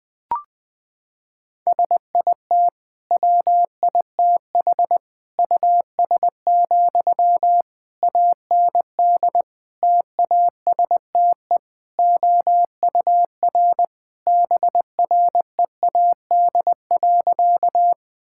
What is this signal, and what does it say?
Morse code, heard as a keyed tone, about 21 wpm, 700 Hz. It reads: SIT WITH US, AND TASTE OUR BREAD.
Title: Sentences from Top 1600 Words 20wpm